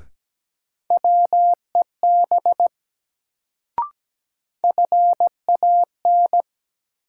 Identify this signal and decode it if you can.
Morse code, heard as a keyed tone, about 17 wpm, 700 Hz. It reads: WEB FAN
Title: Sentences from Top 1000 Words - Review Part 4 - New Word Drill - 17wpm